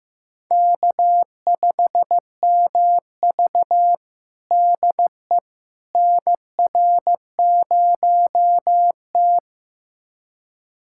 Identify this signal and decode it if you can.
Morse code, heard as a keyed tone, about 15 wpm, 700 Hz. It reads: K5MV DE NR0T
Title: USA Callsign DE Callsign - 4 Characters - 15wpm